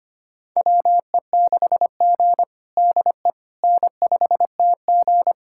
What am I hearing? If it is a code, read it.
Morse code, heard as a keyed tone, about 25 wpm, 700 Hz. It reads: WE6G DE N5TG
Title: USA Callsign DE Callsign - 4 Characters - 25wpm